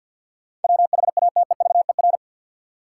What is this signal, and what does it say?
Morse code, heard as a keyed tone, about 47 wpm, 700 Hz. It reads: WHATEVER
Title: Sentences from Top 1100 Words - New Word Drill Alt - 50wpm